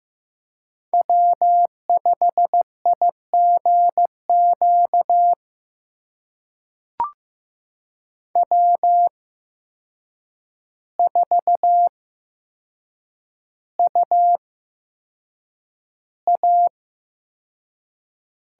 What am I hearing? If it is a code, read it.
Morse code, heard as a keyed tone, about 15 wpm, spaced as for 7 wpm, 700 Hz. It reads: W5IGQ W4UA
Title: USA Call Signs - 8x Character Spacing - 15wpm